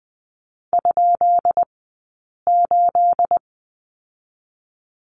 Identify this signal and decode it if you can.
Morse code, heard as a keyed tone, about 20 wpm, spaced as for 8 wpm, 700 Hz. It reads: ?8
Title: 211 Lesson 34d - Number 8 - Two Characters 20wpm